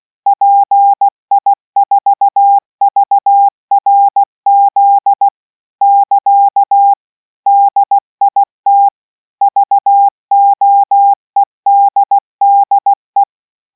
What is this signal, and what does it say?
Morse code, heard as a keyed tone, about 16 wpm, 800 Hz. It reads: PI4VRZ <KA> DIT VOEDDE